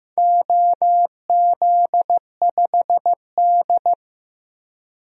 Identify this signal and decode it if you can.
Morse code, heard as a keyed tone, about 15 wpm, 700 Hz. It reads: OZ5D
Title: DXCC-100 Call Signs - 4 Characters - 15wpm